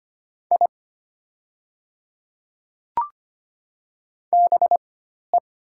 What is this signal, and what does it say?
Morse code, heard as a keyed tone, about 25 wpm, spaced as for 12 wpm, 700 Hz. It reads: I BE
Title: 4x Character Spacing - Top 100 Words - 25wpm